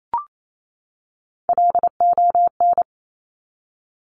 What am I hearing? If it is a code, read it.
Morse code, heard as a keyed tone, about 28 wpm, 700 Hz. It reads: LON
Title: Top Three Letter Combinations in Words 28wpm